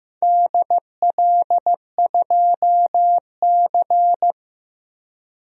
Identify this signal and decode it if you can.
Morse code, heard as a keyed tone, about 15 wpm, 700 Hz. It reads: DL2C
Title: DXCC-100 Call Signs - 4 Characters - Repeated 2x - 15wpm